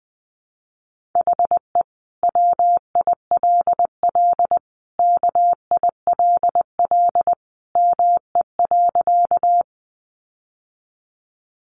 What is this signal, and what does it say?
Morse code, heard as a keyed tone, about 20 wpm, 700 Hz. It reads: HE WILL KILL ME.
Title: Sentences from Top 1400 Words 20wpm